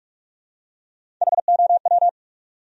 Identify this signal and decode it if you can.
Morse code, heard as a keyed tone, about 43 wpm, 700 Hz. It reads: HOW
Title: Top 100 Words Repeated 45wpm